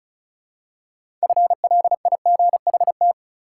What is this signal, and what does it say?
Morse code, heard as a keyed tone, about 34 wpm, 700 Hz. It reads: FLIGHT
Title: Six Letter Words - 35wpm